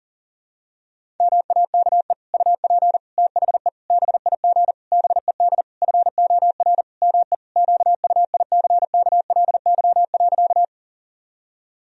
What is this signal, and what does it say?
Morse code, heard as a keyed tone, about 37 wpm, 700 Hz. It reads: MAKE UP THE BIG BED FOR ME QUICKLY.
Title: Sentences from Top 1000 Words - Review Part 1 - 40wpm